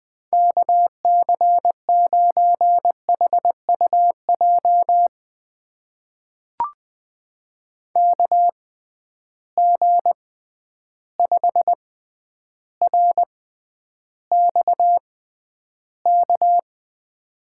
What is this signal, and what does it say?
Morse code, heard as a keyed tone, about 20 wpm, spaced as for 10 wpm, 700 Hz. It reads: KC9HUJ KG5RXK
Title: USA Call Signs - 6x Character Spacing - 20wpm